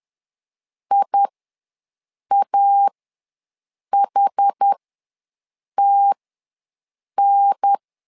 Morse code, heard as a keyed tone, about 10 wpm, spaced as for 6 wpm, 800 Hz. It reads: IAHTN
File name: 2 UHDN ws 10wpm